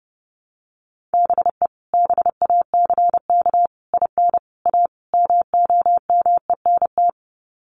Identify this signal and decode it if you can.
Morse code, heard as a keyed tone, about 30 wpm, 700 Hz. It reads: BE BACK IN A MOMENT
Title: Sets of 5 Words 30wpm